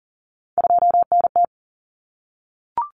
Morse code, heard as a keyed tone, about 40 wpm, 700 Hz. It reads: 2NT
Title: Common Cut Numbers - 3-Digit 40wpm